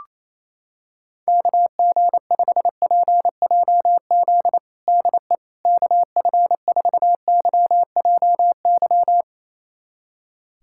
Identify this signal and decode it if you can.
Morse code, heard as a keyed tone, about 28 wpm, 700 Hz. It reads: KG5PJZ DE KF4YJY
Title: USA Callsign DE Callsign - 6 Characters - 28wpm